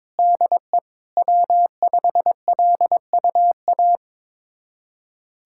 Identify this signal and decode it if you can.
Morse code, heard as a keyed tone, about 22 wpm, 700 Hz. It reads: DE W5LUA
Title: USA Callsign DE Callsign - 5 Characters - 22wpm